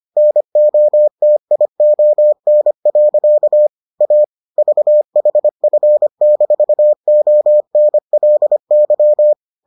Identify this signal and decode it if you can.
Morse code, heard as a keyed tone, about 25 wpm, 600 Hz. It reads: NOTION. A VHF-ONLY